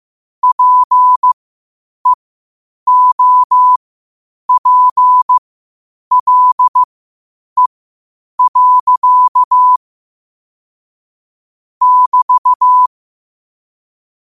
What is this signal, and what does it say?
Morse code, heard as a keyed tone, about 15 wpm, spaced as for 8 wpm, 1000 Hz. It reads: PEOPLE. <BT>
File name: 8wpm